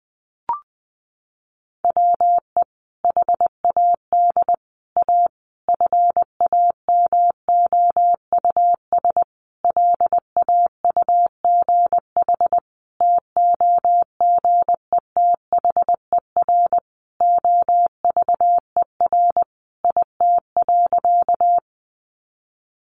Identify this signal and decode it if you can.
Morse code, heard as a keyed tone, about 20 wpm, 700 Hz. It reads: WE HAD A FAMOUS LAUGH TOGETHER OVER IT.
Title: Sentences from Top 1500 Words 20wpm